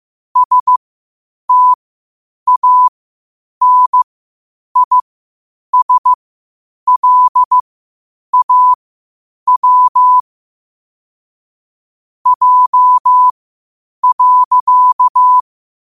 Morse code, heard as a keyed tone, about 15 wpm, spaced as for 8 wpm, 1000 Hz. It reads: STANISLAW J.
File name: qotdfiles / 8wpm